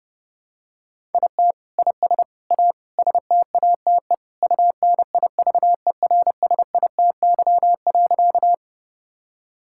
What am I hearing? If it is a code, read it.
Morse code, heard as a keyed tone, about 30 wpm, 700 Hz. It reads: IT IS A STATE UNIVERSITY.
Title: Sentences from Top 1000 Words - Review Part 5 - 30wpm